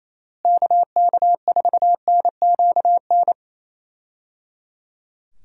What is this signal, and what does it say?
Morse code, heard as a keyed tone, about 28 wpm, 700 Hz. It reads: KK4NQN
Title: USA Call Signs - 6 Characters - Repeated 3x - 28wpm